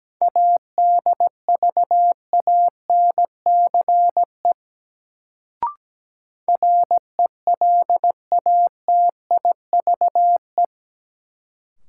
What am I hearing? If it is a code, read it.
Morse code, heard as a keyed tone, about 17 wpm, 700 Hz. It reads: ADVANCE RELATIVE
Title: Sentences from Top 1400 Words - New Word Drill - 17wpm